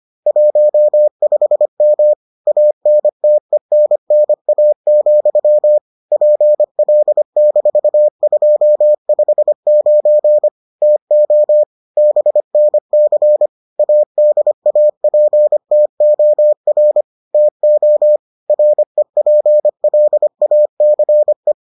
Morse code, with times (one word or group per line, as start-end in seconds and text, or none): 0.3-2.1 15M
2.5-5.8 ANTENNA,
6.1-10.5 PL-259
10.8-11.6 TO
12.0-13.5 BNC
13.8-17.0 ADAPTOR
17.3-18.2 TO
18.5-21.5 REPLACE